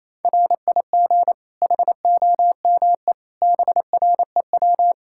Morse code, heard as a keyed tone, about 28 wpm, 700 Hz. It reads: RIG HOME BREW
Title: Practice QSO - Exchange 2 - Info - 28wpm